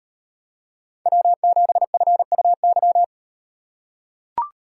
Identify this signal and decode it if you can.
Morse code, heard as a keyed tone, about 36 wpm, 700 Hz. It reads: W7FUY